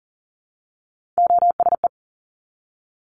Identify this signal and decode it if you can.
Morse code, heard as a keyed tone, about 37 wpm, 700 Hz. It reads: OSE
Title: Top Three Letter Combinations in Words 40wpm